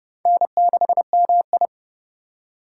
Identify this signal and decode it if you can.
Morse code, heard as a keyed tone, about 31 wpm, 700 Hz. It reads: N6MI